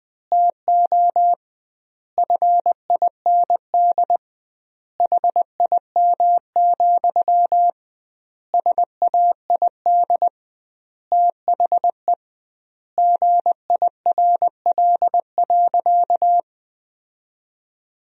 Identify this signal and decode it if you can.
Morse code, heard as a keyed tone, about 21 wpm, 700 Hz. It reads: TO FIND HIM, SAID THE GIRL.
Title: 2x Word-Spacing - Sentences from Top 400 Words - 20wpm